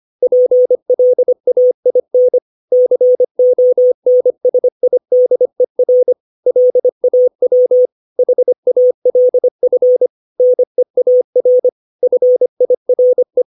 Morse code, heard as a keyed tone, about 25 wpm, 500 Hz. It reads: PLAIN CONSIDER LAW HALF NEAR FIRE
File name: 632 words / 632 Words 25WPM